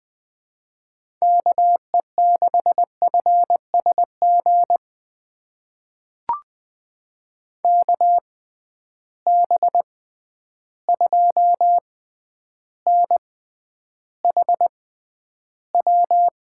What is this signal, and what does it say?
Morse code, heard as a keyed tone, about 20 wpm, spaced as for 10 wpm, 700 Hz. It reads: KE6FSG KB2NHW
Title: USA Call Signs - 6 Characters - 6x Character Spacing - 20wpm